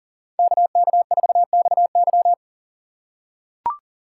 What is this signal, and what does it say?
Morse code, heard as a keyed tone, about 40 wpm, 700 Hz. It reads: KK4XY
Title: USA Call Signs - 2x Character Spacing - 40wpm